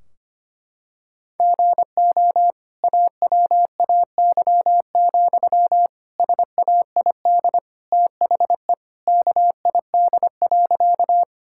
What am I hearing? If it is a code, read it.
Morse code, heard as a keyed tone, about 25 wpm, 700 Hz. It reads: GO AWAY, SAID THE KID.
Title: Sentences from Top 1600 Words 25wpm